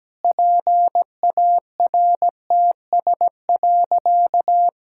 Morse code, heard as a keyed tone, about 17 wpm, 700 Hz. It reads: PARTS.
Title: Sentences from Top 1000 Words - Review Part 6 - 17wpm